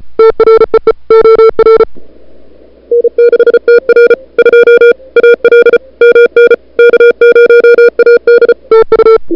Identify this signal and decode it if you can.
Morse code, heard as a keyed tone, about 34 wpm, 440 Hz. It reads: TREE OR N6TR 2 AL MN K0AD TU
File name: n6tr / Fab2007CWSprintHour1